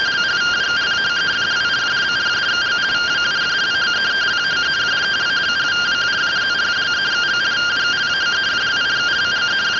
Two-tone teleprinter signal, about 170 Hz shift.